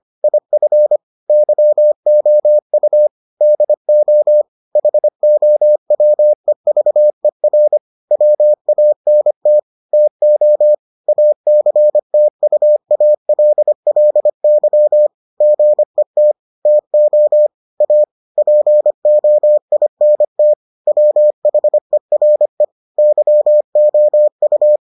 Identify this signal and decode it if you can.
Morse code, heard as a keyed tone, about 25 wpm, 600 Hz. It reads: IF YOU DO HOWEVER WANT TO ACTUALLY GET TO A POINT WHERE YOU